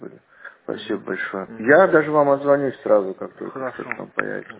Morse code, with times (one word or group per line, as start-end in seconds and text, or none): none